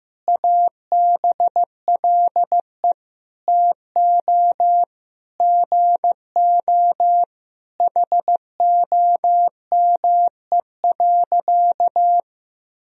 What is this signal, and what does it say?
Morse code, heard as a keyed tone, about 15 wpm, 700 Hz. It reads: ABLE TO GO HOME.
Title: Sentences from Top 1000 Words - Review Part 3 - 15wpm